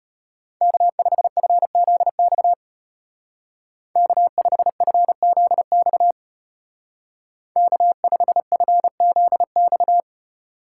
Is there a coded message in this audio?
Yes: Morse code from 0.6 to 10.0 s, heard as a keyed tone, about 33 wpm, 700 Hz. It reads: K5FZX K5FZX K5FZX